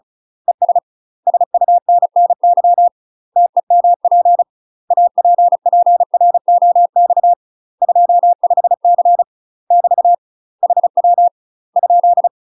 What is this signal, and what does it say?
Morse code, heard as a keyed tone, about 36 wpm, 700 Hz. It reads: ES SUNNY TEMP APPROX 25C <BT> HW ?
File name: UK MASTER QSOs TRAINING / 35WPM / UK MASTER QSOs 0011